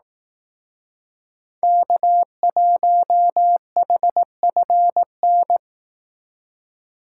Morse code, heard as a keyed tone, about 18 wpm, 700 Hz. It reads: K1HFN
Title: Speed Racing - USA 1x3 Call Signs 15wpm